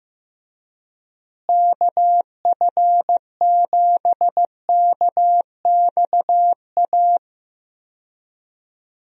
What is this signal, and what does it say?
Morse code, heard as a keyed tone, about 16 wpm, 700 Hz. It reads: KF7KXA
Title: USA Call Signs - 6 Characters - Repeated 3x - 15wpm